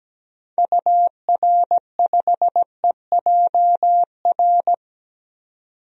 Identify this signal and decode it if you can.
Morse code, heard as a keyed tone, about 17 wpm, 700 Hz. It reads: UR5EJR
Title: DXCC-100 Call Signs - 6 Characters - 17wpm